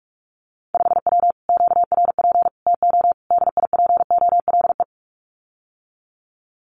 Morse code, heard as a keyed tone, about 46 wpm, 700 Hz. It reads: 5W QRP TO DIPOLE
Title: Common QSO Phrases Encore 45wpm